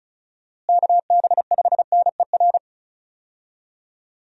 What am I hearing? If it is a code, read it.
Morse code, heard as a keyed tone, about 36 wpm, 700 Hz. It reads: KB5NER
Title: USA Call Signs - 6 Characters - Repeated 2x - 35wpm